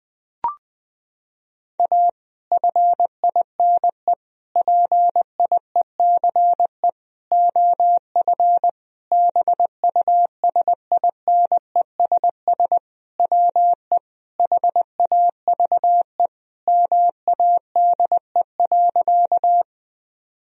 Morse code, heard as a keyed tone, about 20 wpm, 700 Hz. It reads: A FINE PIECE OF BUSINESS WE HAVE MADE.
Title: Sentences from Top 1000 Words - Review Part 2 - 20wpm